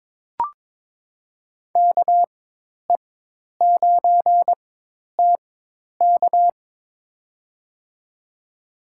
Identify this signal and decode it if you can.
Morse code, heard as a keyed tone, about 23 wpm, spaced as for 10 wpm, 700 Hz. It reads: KE9TK
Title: USA Call Signs - 4x Character Spacing - 22wpm